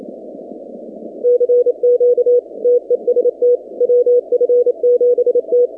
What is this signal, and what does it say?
Morse code, heard as a keyed tone, about 27 wpm, 490 Hz. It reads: CQ TEST WF7T
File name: 2019-cw-wpx-kt5j-k5pi